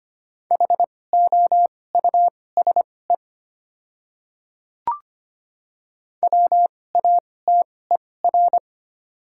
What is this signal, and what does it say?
Morse code, heard as a keyed tone, about 25 wpm, spaced as for 18 wpm, 700 Hz. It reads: HOUSE WATER
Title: 2x Character Spacing - Top 200 Words - 25wpm